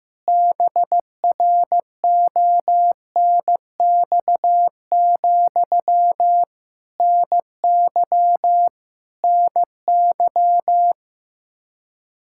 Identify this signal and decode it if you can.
Morse code, heard as a keyed tone, about 15 wpm, 700 Hz. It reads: BRONX, NY NY
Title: Practice QSO - Exchange 1 - Beginning - 15wpm